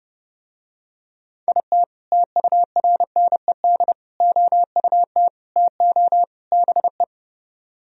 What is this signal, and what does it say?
Morse code, heard as a keyed tone, about 31 wpm, 700 Hz. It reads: IT TURNED OUT TO BE